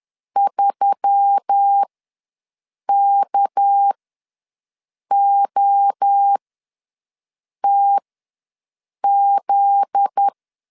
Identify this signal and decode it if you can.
Morse code, heard as a keyed tone, about 10 wpm, spaced as for 6 wpm, 800 Hz. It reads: 3KOTZ
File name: LEZIONI / 7 GF9Z / 7 GF9Z ws 10wpm 3919